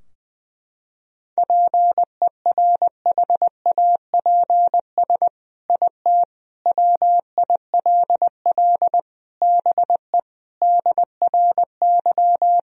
Morse code, heard as a keyed tone, about 21 wpm, 700 Hz. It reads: PERHAPS IT WILL BE DRY
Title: Sentences from Top 1000 Words - Review Part 3 - 20wpm